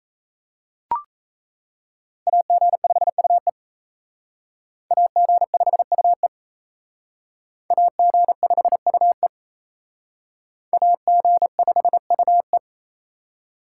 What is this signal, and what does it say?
Morse code, heard as a keyed tone, about 36 wpm, 700 Hz. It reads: AG5UE AG5UE AG5UE AG5UE